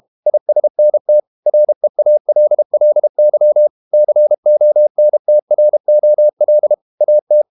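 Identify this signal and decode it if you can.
Morse code, heard as a keyed tone, about 32 wpm, 600 Hz. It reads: ISNT REALLY CONTROL AT
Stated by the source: Morse code practice at 32 wpm from CW med Gnister, reading Ken Follett's novel Never in English.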